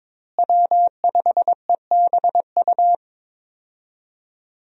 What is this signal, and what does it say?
Morse code, heard as a keyed tone, about 22 wpm, 700 Hz. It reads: W5EBU
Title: USA Call Signs - 5 Characters - Repeated 2x - 22wpm